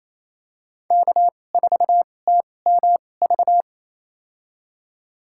Morse code, heard as a keyed tone, about 28 wpm, spaced as for 20 wpm, 700 Hz. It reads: K4TMV
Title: USA Call Signs - 5 Characters - 2x Character Spacing - 28wpm